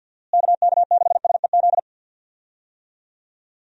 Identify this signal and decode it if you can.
Morse code, heard as a keyed tone, about 50 wpm, 700 Hz. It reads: KK6SEZ
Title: USA Call Signs - 6 Characters - Repeated 2x - 50wpm